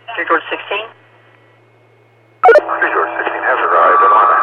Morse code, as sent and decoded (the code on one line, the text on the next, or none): none